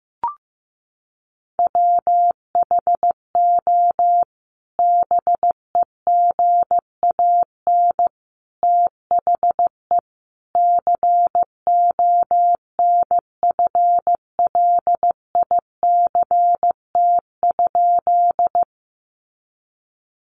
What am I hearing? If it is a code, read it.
Morse code, heard as a keyed tone, about 15 wpm, 700 Hz. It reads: WHO BEGAN THE CONFLICT?
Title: Sentences from Top 1300 Words 15wpm